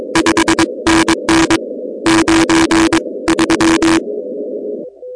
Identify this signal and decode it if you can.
Morse code, heard as a keyed tone, about 22 wpm, 330 Hz, with noise. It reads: 5NN 93
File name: contests / 2019-cw-wpx-kt5j-k5pi- -2330 0-16